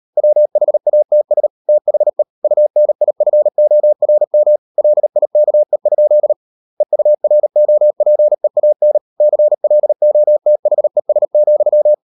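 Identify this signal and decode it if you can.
Morse code, heard as a keyed tone, about 38 wpm, 600 Hz. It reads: WHATS THE UNIFORM LIKE? EUROPEAN CLOTHES,
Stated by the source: Morse code practice at 38 WPM, sending Ken Follett's novel Never in English, from CW med Gnister.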